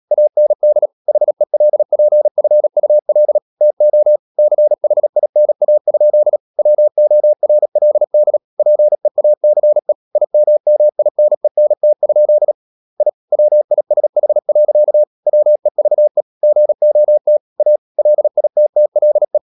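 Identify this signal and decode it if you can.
Morse code, heard as a keyed tone, about 37 wpm, 600 Hz. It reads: AND HELPFUL TO CHINA? WORLD PEACE IMMINENT? I WISH. WEVE GOT A LITTLE
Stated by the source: Morse practice at 36 wpm, text of Ken Follett's novel Never, from CW med Gnister.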